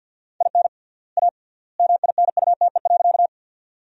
Morse code, heard as a keyed tone, about 53 wpm, 700 Hz. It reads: IN A MINUTE.
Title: 3x Word-Spacing - Sentences from Top 500 Words - 50wpm